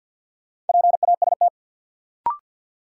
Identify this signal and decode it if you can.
Morse code, heard as a keyed tone, about 52 wpm, 700 Hz. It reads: PAST
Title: Sentences from Top 600 Words - New Word Drill - 50wpm